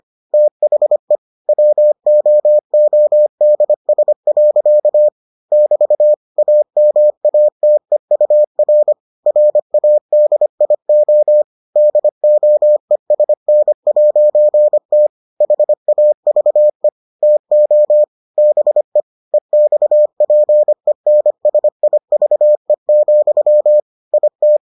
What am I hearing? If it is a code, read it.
Morse code, heard as a keyed tone, about 25 wpm, 600 Hz. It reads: THE WOODS. <BT> AMATEUR RADIO DOESN'T HAVE TO BE EXPENSIVE, IT